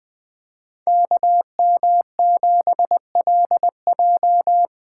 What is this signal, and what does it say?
Morse code, heard as a keyed tone, about 24 wpm, 700 Hz. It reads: KM7LJ
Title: Speed Racing - USA 2x2 Call Signs 20wpm